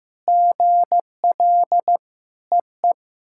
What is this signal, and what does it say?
Morse code, heard as a keyed tone, about 15 wpm, 700 Hz. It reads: GL EE
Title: SOTA QSO Practice 15wpm